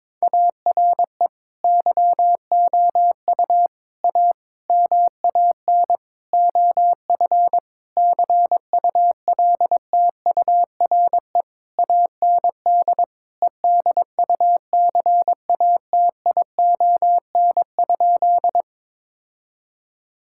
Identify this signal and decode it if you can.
Morse code, heard as a keyed tone, about 21 wpm, 700 Hz. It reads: ARE YOU A MAN OF CULTURE AND EDUCATION?
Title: Sentences from Top 1000 Words - Review Part 6 - 22wpm